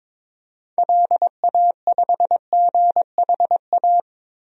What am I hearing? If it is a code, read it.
Morse code, heard as a keyed tone, about 21 wpm, 700 Hz. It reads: LA5GHA
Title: DXCC-100 Call Signs - 6 Characters - Repeated 3x - 22wpm